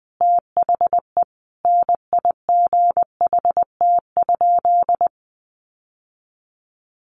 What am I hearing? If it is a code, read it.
Morse code, heard as a keyed tone, about 20 wpm, 700 Hz. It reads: THE NIGHT?
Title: Sentences from Top 1300 Words 20wpm